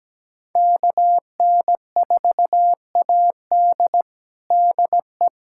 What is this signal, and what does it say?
Morse code, heard as a keyed tone, about 17 wpm, 700 Hz. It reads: KN4AD DE
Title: USA Callsign DE Callsign - 5 Characters - 17wpm